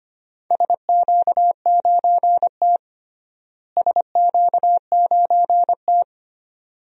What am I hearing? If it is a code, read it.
Morse code, heard as a keyed tone, about 25 wpm, 700 Hz. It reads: SQ9T SQ9T
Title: DXCC-100 Call Signs - 4 Characters - Repeated 2x - 25wpm